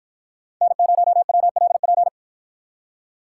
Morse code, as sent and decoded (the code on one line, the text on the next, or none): -. ----- .-- .-.. .--.
N0WLP